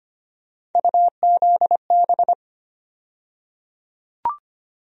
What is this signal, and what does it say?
Morse code, heard as a keyed tone, about 25 wpm, 700 Hz. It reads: UZB
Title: USA Call Sign Postfixes - 3 Letters 25wpm